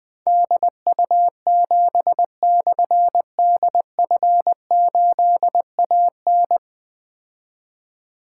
Morse code, heard as a keyed tone, about 21 wpm, 700 Hz. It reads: DU7/DF8AN
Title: Difficult Call Signs 20wpm